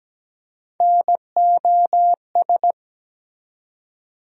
Morse code, heard as a keyed tone, about 17 wpm, 700 Hz. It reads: NOS
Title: USA Call Sign Postfixes - 3 Letters 17wpm